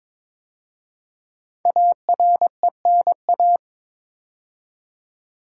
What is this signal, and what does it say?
Morse code, heard as a keyed tone, about 22 wpm, 700 Hz. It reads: ARENA
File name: 5 Letter Words - First 3 Letters Given - 22wpm